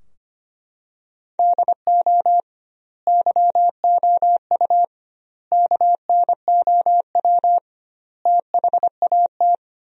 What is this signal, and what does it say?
Morse code, heard as a keyed tone, about 25 wpm, 700 Hz. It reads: DO YOU KNOW THAT
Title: 2x Word-Spacing - Sentences from Top 400 Words - 25wpm